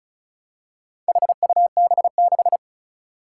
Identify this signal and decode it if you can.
Morse code, heard as a keyed tone, about 36 wpm, 700 Hz. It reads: HUB6
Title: Four Challenging Letter-Number Combinations 35wpm